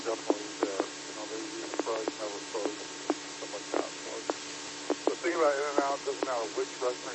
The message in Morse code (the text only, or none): none